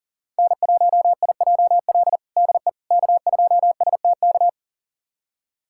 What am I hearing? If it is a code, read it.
Morse code, heard as a keyed tone, about 40 wpm, 700 Hz. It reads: N1IJL DE K2STK
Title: USA Callsign DE Callsign - 5 Characters - 40wpm